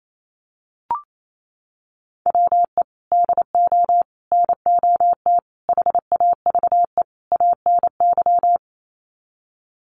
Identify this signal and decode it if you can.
Morse code, heard as a keyed tone, about 28 wpm, 700 Hz. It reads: WE DO NOT HAVE ANY
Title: Sets of 5 Words 28wpm